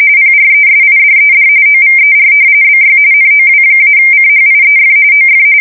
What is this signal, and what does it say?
RTTY, about 45.45 baud. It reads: PRESIDED OVER BY NOW-CHIEF OF
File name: fsk170 / battleshipnj